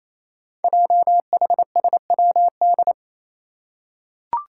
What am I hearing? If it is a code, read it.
Morse code, heard as a keyed tone, about 28 wpm, 700 Hz. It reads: JHSWD
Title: Code Groups - 5 Letters - 28wpm